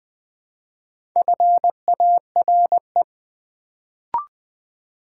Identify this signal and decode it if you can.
Morse code, heard as a keyed tone, about 20 wpm, 690 Hz. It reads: FARE
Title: Build A Word - 1 Letter At A Time - 4-Letter Words - 20wpm